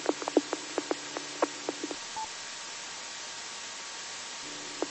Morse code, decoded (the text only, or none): none